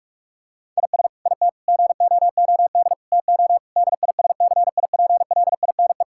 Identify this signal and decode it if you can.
Morse code, heard as a keyed tone, about 46 wpm, 700 Hz. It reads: IS IT GOOD TO DISCIPLINE
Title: Sentences from Top 1000 Words - Review Part 6 - 45wpm